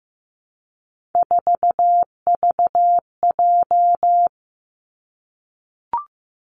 Morse code, heard as a keyed tone, about 15 wpm, 700 Hz. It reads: 4VJ
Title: Three Challenging Letter-Number Combinations 15wpm